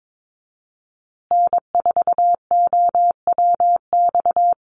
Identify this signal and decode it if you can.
Morse code, heard as a keyed tone, about 22 wpm, 700 Hz. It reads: N4OWX
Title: USA Call Signs Repeated 3x 22wpm